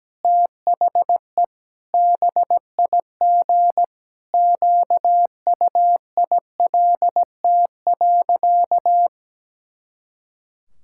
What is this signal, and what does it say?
Morse code, heard as a keyed tone, about 17 wpm, 700 Hz. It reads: THE BIG QUILT.
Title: Pangram Challenge v2 - 17wpm